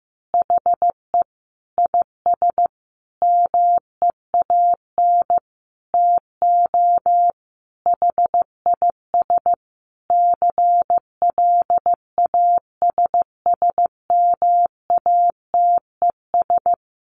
Morse code, heard as a keyed tone, about 15 wpm, 700 Hz. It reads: HE IS MEAN TO HIS CLASSMATES
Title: Additional Easy Phrases 15wpm